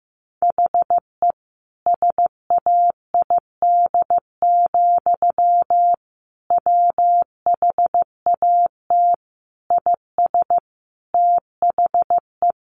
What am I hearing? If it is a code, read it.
Morse code, heard as a keyed tone, about 15 wpm, 700 Hz. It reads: HE SAID, WHAT IS THE
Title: Sentences from Top 700 Words 15wpm